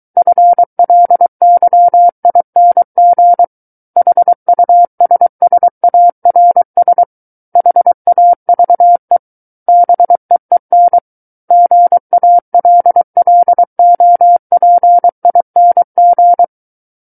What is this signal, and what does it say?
Morse code, heard as a keyed tone, about 23 wpm, 700 Hz. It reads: FLYING HUSSARS HAVE BEEN GALLOPING
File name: Book One / 14-Chapter 14 - 23 WPM